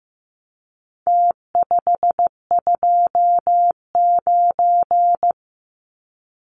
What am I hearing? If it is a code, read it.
Morse code, heard as a keyed tone, about 15 wpm, 700 Hz. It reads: T529
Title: Numbers - Contest Style - 4-Digit 15wpm